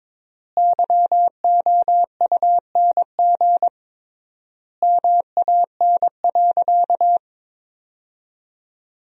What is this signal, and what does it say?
Morse code, heard as a keyed tone, about 21 wpm, 700 Hz. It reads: YOUNG MAN.
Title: 3x Word-Spacing - Sentences from Top 400 Words - 22wpm